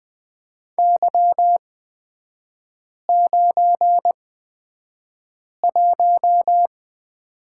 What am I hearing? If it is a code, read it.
Morse code, heard as a keyed tone, about 20 wpm, spaced as for 5 wpm, 700 Hz. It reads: Y91
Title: 255 Lesson 41e - Period - Three Characters 20wpm